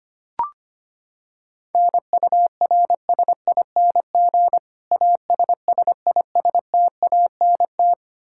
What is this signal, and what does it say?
Morse code, heard as a keyed tone, about 25 wpm, 700 Hz. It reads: NURSING ASSISTANT
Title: Practice QSO - Exchange 3 - More Info - 25wpm